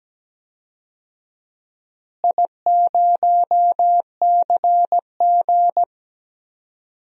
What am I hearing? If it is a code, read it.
Morse code, heard as a keyed tone, about 17 wpm, 700 Hz. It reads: I0CG
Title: DXCC-100 Call Signs - 4 Characters - Repeated 2x - 17wpm